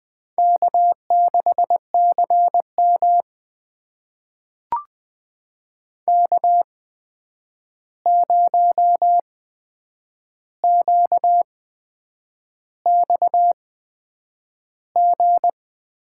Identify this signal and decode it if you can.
Morse code, heard as a keyed tone, about 20 wpm, spaced as for 8 wpm, 700 Hz. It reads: K6CM K0QXG
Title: USA Call Signs - 8x Character Spacing - 20wpm